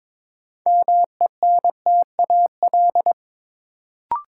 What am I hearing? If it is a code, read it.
Morse code, heard as a keyed tone, about 22 wpm, 700 Hz. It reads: MENTAL
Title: Sentences from Top 1000 Words - Review Part 4 - New Word Drill - 22wpm